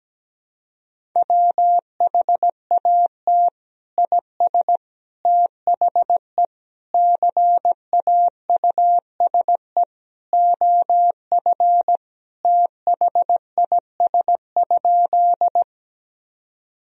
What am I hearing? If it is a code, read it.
Morse code, heard as a keyed tone, about 17 wpm, 700 Hz. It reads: WHAT IS THE CAUSE OF THIS?
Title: Sentences from Top 1000 Words - Review Part 1 - 17wpm